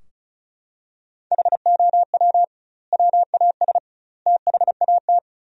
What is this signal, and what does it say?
Morse code, heard as a keyed tone, about 36 wpm, 700 Hz. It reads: HOW WAS THAT
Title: 2x Word-Spacing - Sentences from Top 500 Words - 35wpm